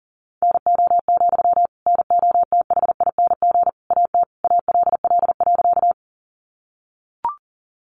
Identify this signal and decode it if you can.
Morse code, heard as a keyed tone, about 40 wpm, 700 Hz. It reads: NO, NOTHING AT ALL.
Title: Sentences from Top 500 Words 40wpm